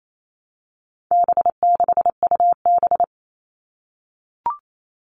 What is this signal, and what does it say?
Morse code, heard as a keyed tone, about 28 wpm, 700 Hz. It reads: B6UB
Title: All Cut Numbers - 4-Digit 28wpm